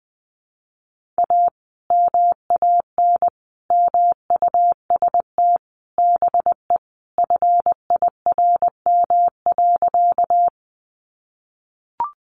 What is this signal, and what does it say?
Morse code, heard as a keyed tone, about 19 wpm, 700 Hz. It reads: A MAN MUST BE FIRM.
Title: Sentences from Top 1400 Words 20wpm